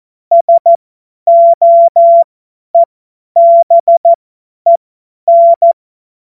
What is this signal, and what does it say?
Morse code, heard as a keyed tone, about 14 wpm, spaced as for 10 wpm, 680 Hz. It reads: SOEBEN
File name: Morsekurs / Witze / Witz 09. November 2021, 50 F70